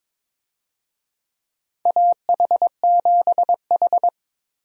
Morse code, heard as a keyed tone, about 22 wpm, 700 Hz. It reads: AH7H
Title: USA Call Signs - 4 Characters - Repeated 3x - 22wpm